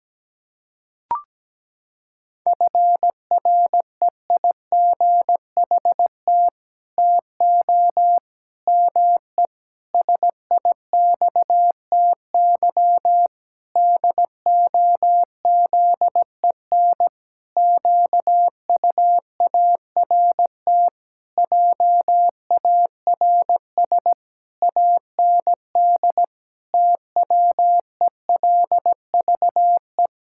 Morse code, heard as a keyed tone, about 17 wpm, 700 Hz. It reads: FREIGHT TO ME SIXTY DOZEN QUART JARS AND TWELVE
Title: Pangram Challenge v2 - 17wpm